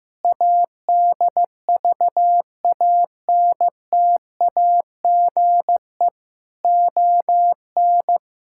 Morse code, heard as a keyed tone, about 16 wpm, 700 Hz. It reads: ADVANTAGE ON
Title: Sentences from Top 1000 Words - Review Part 6 - 15wpm